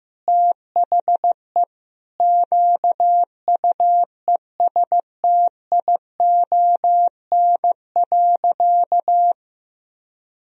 Morse code, heard as a keyed tone, about 15 wpm, 700 Hz. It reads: THE QUESTION.
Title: Sentences from Top 1000 Words - Review Part 1 - 15wpm